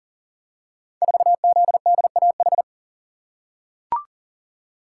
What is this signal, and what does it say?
Morse code, heard as a keyed tone, about 38 wpm, 690 Hz. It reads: 4ZDAH
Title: Code Groups - 5 Letter:Numbers - 40wpm